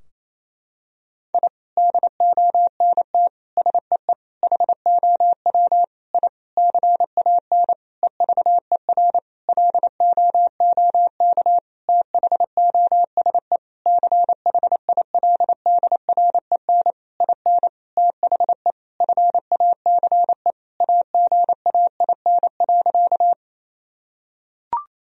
Morse code, heard as a keyed tone, about 28 wpm, 700 Hz. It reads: I DONT SEE HOW I CAN EVER LOOK THOSE CHILDREN IN THE FACE AGAIN.
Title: Sentences from Top 1000 Words - Review Part 2 - 28wpm